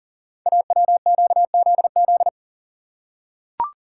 Morse code, heard as a keyed tone, about 40 wpm, 690 Hz. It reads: AWQZZ
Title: Code Groups - 5 Letters - 40wpm